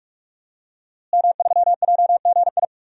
Morse code, heard as a keyed tone, about 46 wpm, 700 Hz. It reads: M3JGI
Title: DXCC-100 Call Signs - 5 Characters - Repeated 2x - 45wpm